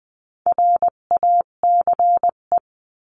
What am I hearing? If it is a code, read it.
Morse code, heard as a keyed tone, about 20 wpm, 700 Hz. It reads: RACE
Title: 258 Lesson 42c - Words 16-20wpm